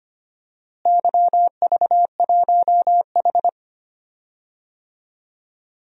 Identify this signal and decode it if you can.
Morse code, heard as a keyed tone, about 25 wpm, 700 Hz. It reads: YV1H
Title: DXCC-100 Call Signs - 4 Characters - Repeated 2x - 25wpm